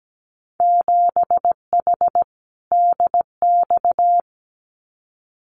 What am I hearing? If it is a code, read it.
Morse code, heard as a keyed tone, about 17 wpm, 700 Hz. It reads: 7H DX
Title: Winter Field Day Exchanges Encore 17wpm